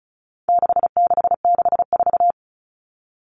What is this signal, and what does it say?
Morse code, heard as a keyed tone, about 34 wpm, 700 Hz. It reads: 6664
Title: Numbers - 4-Digit 35wpm